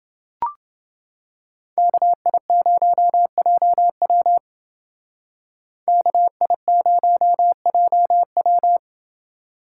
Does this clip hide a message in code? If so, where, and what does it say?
Yes: Morse code from 1.8 to 8.8 s, heard as a keyed tone, about 29 wpm, 700 Hz. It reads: KI0JW KI0JW